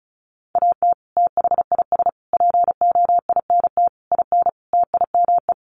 Morse code, heard as a keyed tone, about 34 wpm, 700 Hz. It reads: AT THIS POINT IN TIME
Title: Sets of 5 Words Encore 35wpm